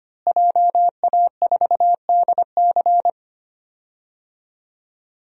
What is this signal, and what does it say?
Morse code, heard as a keyed tone, about 25 wpm, 700 Hz. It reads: JA4DC
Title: DXCC-100 Call Signs - 5 Characters - Repeated 2x - 25wpm